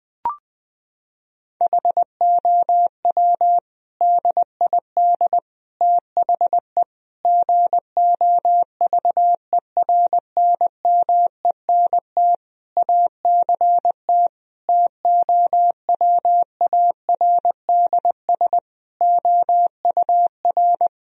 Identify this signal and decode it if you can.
Morse code, heard as a keyed tone, about 21 wpm, 700 Hz. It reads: HOW DID THE GOVERNMENT ACT TOWARDS OUR
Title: Sentences from Top 1000 Words - Review Part 4 - 20wpm